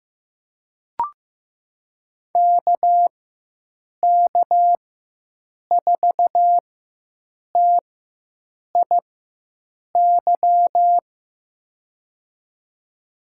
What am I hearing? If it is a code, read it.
Morse code, heard as a keyed tone, about 15 wpm, spaced as for 7 wpm, 700 Hz. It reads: KK4TIY
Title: USA Call Signs - 4x Character Spacing - 15wpm